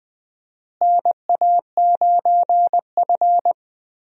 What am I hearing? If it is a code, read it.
Morse code, heard as a keyed tone, about 20 wpm, 700 Hz. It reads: NA9F
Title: USA Call Signs - 4 Characters - Repeated 2x - 20wpm